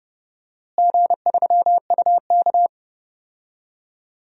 Morse code, heard as a keyed tone, about 30 wpm, 700 Hz. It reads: G3UK
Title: DXCC-100 Call Signs - 4 Characters - Repeated 3x - 30wpm